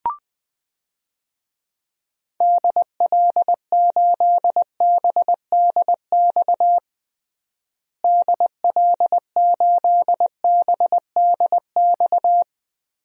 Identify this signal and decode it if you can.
Morse code, heard as a keyed tone, about 20 wpm, 700 Hz. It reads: DL8BDX DL8BDX
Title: DXCC-100 Call Signs - 6 Characters - Repeated 3x - 20wpm